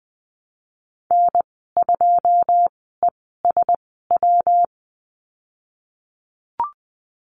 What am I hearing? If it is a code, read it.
Morse code, heard as a keyed tone, about 20 wpm, spaced as for 15 wpm, 700 Hz. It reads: N2ESW
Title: 154 Lesson 25g - Letter p - Call Signs 20wpm